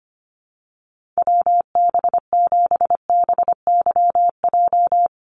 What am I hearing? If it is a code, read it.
Morse code, heard as a keyed tone, about 25 wpm, 700 Hz. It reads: WB7BYJ
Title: USA Call Signs 25wpm